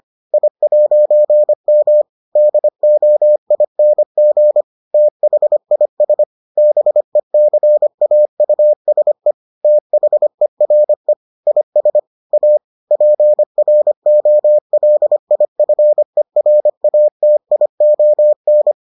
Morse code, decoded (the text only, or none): I'M DOING THIS BECAUSE THERE IS A PROLIFERATION